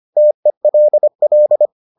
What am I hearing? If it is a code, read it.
Morse code, heard as a keyed tone, about 25 wpm, 600 Hz. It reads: TELL